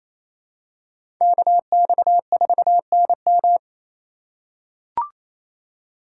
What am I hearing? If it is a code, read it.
Morse code, heard as a keyed tone, about 28 wpm, 700 Hz. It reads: KX4NM
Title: K1USN SST QSO Practice - 28wpm